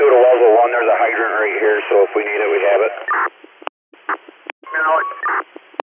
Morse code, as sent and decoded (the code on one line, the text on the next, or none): none